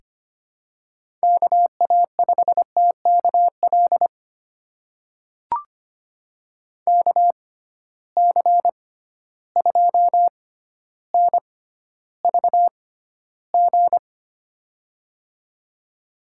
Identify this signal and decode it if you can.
Morse code, heard as a keyed tone, about 25 wpm, spaced as for 13 wpm, 700 Hz. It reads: KA5TKL KC2NVG
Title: USA Call Signs - 6 Characters - 6x Character Spacing - 25wpm